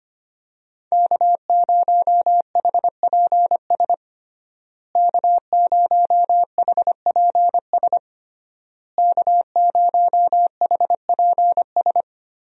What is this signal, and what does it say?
Morse code, heard as a keyed tone, about 25 wpm, 700 Hz. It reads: K0HPS K0HPS K0HPS
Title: USA Call Signs - 5 Characters - Repeated 3x - 25wpm